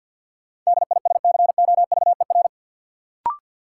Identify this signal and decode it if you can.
Morse code, heard as a keyed tone, about 53 wpm, 700 Hz. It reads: DISCOVER
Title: Sentences from Top 1100 Words - New Word Drill Alt - 50wpm